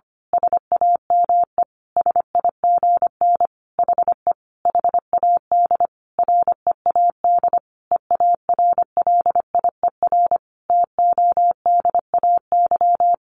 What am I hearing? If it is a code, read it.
Morse code, heard as a keyed tone, about 25 wpm, 700 Hz. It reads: SAME SIGN HE HAD READ EARLIER TODAY
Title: A Sound Of Thunder 25wpm